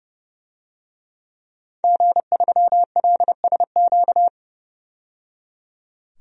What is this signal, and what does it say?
Morse code, heard as a keyed tone, about 30 wpm, 700 Hz. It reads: G3LSQ
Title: DXCC-100 Call Signs - 5 Characters - 30wpm